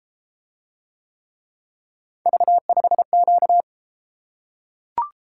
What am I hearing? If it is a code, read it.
Morse code, heard as a keyed tone, about 33 wpm, 700 Hz. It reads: V5Q